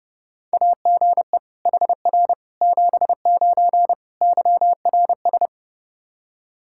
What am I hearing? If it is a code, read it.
Morse code, heard as a keyed tone, about 32 wpm, 700 Hz. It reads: AGE HR 79 YRS
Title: Practice QSO - Exchange 3 - More Info - 30wpm